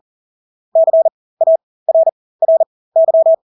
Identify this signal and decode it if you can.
Morse code, heard as a keyed tone, about 40 wpm, spaced as for 20 wpm, 650 Hz. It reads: CARRY
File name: Engelske ord 40-20 wpm